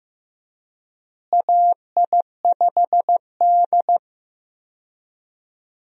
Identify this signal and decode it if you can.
Morse code, heard as a keyed tone, about 15 wpm, 700 Hz. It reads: AI5D
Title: USA Call Signs - 4 Characters - Repeated 3x - 15wpm